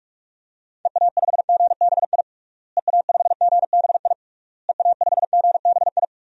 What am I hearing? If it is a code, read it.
Morse code, heard as a keyed tone, about 46 wpm, 700 Hz. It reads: EA5GBI EA5GBI EA5GBI
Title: DXCC-100 Call Signs - 6 Characters - Repeated 3x - 45wpm